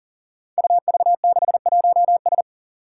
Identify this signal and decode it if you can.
Morse code, heard as a keyed tone, about 40 wpm, 700 Hz. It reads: UV61S
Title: Five Challenging Letter-Number Combinations 40wpm